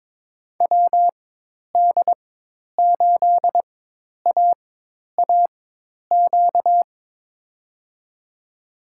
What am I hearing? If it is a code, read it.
Morse code, heard as a keyed tone, about 22 wpm, spaced as for 10 wpm, 700 Hz. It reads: WD8AAQ
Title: USA Call Signs - 6 Characters - 4x Character Spacing - 22wpm